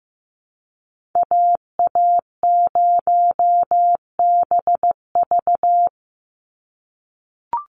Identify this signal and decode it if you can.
Morse code, heard as a keyed tone, about 15 wpm, 700 Hz. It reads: AA0BV
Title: NA SOTA Activator Callsigns 15wpm